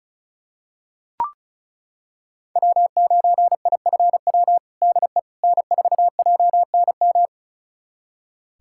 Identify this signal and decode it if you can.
Morse code, heard as a keyed tone, about 36 wpm, 700 Hz. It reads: W9IFW DE N4JNM
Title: USA Callsign DE Callsign - 5 Characters - 35wpm